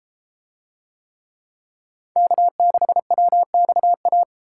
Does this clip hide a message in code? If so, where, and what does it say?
Yes: Morse code from 2.2 to 4.2 s, heard as a keyed tone, about 32 wpm, 700 Hz. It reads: K6WXA